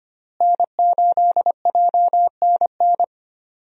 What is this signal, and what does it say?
Morse code, heard as a keyed tone, about 25 wpm, 700 Hz. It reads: N8JNN
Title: USA Call Signs - 5 Characters - Repeated 3x - 25wpm